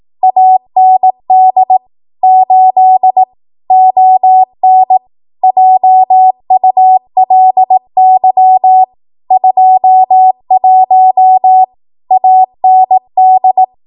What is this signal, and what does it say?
Morse code, heard as a keyed tone, about 18 wpm, 750 Hz. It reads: AND 8 ON JULY 21 AND
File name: Archive / 18 WPM / 18WPM